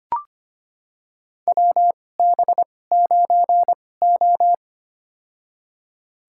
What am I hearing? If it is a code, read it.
Morse code, heard as a keyed tone, about 25 wpm, spaced as for 18 wpm, 700 Hz. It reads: WB9O
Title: USA Call Signs - 4 Characters - 2x Character Spacing - 25wpm